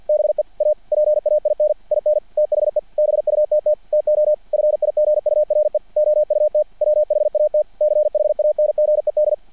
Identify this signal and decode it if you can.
Morse code, heard as a keyed tone, about 50 wpm, 600 Hz. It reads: BE A PAIN IN THE BUTT TO FIGURE OUT WHAT CHANGED
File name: archived / CRSnet